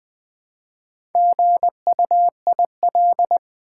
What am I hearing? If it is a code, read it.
Morse code, heard as a keyed tone, about 20 wpm, 700 Hz. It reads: GUIL
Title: Build A Word - 2 Letters At A Time - 6-Letter Words - 20wpm